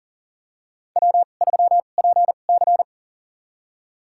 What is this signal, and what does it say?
Morse code, heard as a keyed tone, about 40 wpm, spaced as for 29 wpm, 700 Hz. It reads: W3PC
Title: USA Call Signs - 2x Character Spacing - 40wpm